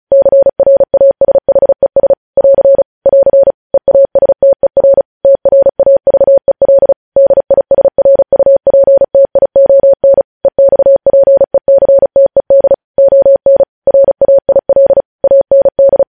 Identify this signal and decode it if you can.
Morse code, heard as a keyed tone, about 36 wpm, 550 Hz. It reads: CRASHES <AR> <AR> EASTER TRAVEL DISRUPTION EXPECTED ON RAIL AND
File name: podcast / News headlines-1x-35x35